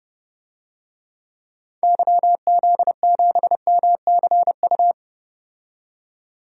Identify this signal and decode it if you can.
Morse code, heard as a keyed tone, about 30 wpm, 700 Hz. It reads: YZ7MCU
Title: DXCC-100 Call Signs - 6 Characters - 30wpm